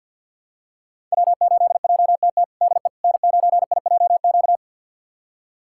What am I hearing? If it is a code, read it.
Morse code, heard as a keyed tone, about 46 wpm, 700 Hz. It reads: W8JTT DE N9IJQ
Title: USA Callsign DE Callsign - 5 Characters - 50wpm